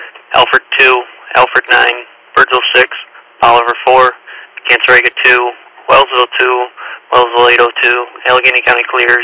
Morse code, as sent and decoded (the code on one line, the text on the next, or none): none